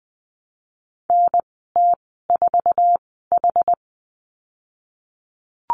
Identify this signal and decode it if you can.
Morse code, heard as a keyed tone, about 20 wpm, spaced as for 14 wpm, 700 Hz. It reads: NT4H
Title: 122 Lesson 20g - Number 3 - Call Signs 20wpm